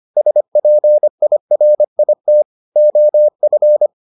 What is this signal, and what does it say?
Morse code, heard as a keyed tone, about 25 wpm, 600 Hz. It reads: SPIRIT OF